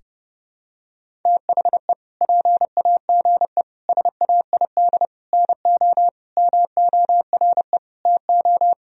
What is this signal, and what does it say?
Morse code, heard as a keyed tone, about 32 wpm, 700 Hz. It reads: THE PAGE SAID NO MORE TO
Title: Sentences from Top 1000 Words - Review Part 1 - 30wpm